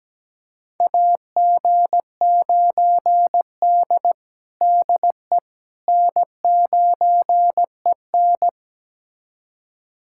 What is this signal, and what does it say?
Morse code, heard as a keyed tone, about 17 wpm, 700 Hz. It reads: AG9D DE N9EN
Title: USA Callsign DE Callsign - 4 Characters - 17wpm